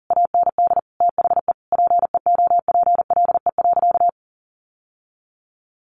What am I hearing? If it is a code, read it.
Morse code, heard as a keyed tone, about 40 wpm, 700 Hz. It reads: AND THE PEOPLE.
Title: Sentences from Top 1500 Words 40wpm